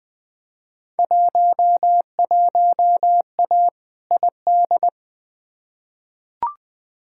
Morse code, heard as a keyed tone, about 20 wpm, 700 Hz. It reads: 11A ID
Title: Field Day Exchanges Encore - 20wpm